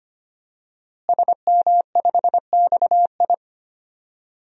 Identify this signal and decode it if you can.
Morse code, heard as a keyed tone, about 25 wpm, 700 Hz. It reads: SM5XI
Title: DXCC-100 Call Signs - 5 Characters - Repeated 2x - 25wpm